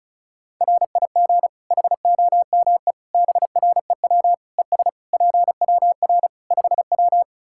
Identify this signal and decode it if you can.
Morse code, heard as a keyed tone, about 36 wpm, 700 Hz. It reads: RIG HOME BREW ES PWR 5W
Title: K1USN SST QSO Practice 35wpm